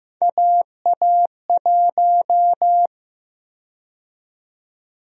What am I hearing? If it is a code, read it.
Morse code, heard as a keyed tone, about 16 wpm, 700 Hz. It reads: AA1
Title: USA Call Sign Prefixes - 15wpm